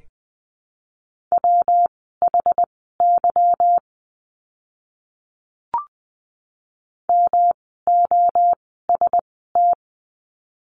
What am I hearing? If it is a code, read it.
Morse code, heard as a keyed tone, about 20 wpm, spaced as for 15 wpm, 700 Hz. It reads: WHY MOST
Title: 237 Lesson 38f - Letter q - Words 20wpm